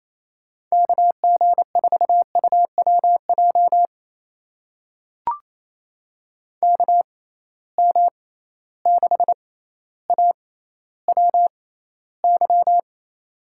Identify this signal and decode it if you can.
Morse code, heard as a keyed tone, about 28 wpm, spaced as for 14 wpm, 700 Hz. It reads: KG4UWJ KM6AWY
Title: USA Call Signs - 6 Characters - 6x Character Spacing - 28wpm